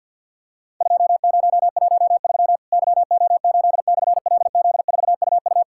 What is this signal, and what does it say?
Morse code, heard as a keyed tone, about 50 wpm, 700 Hz. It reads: 2013 YO8/LZ4UU
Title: Worked All Europe DX Contest - QTC Exchange - 50wpm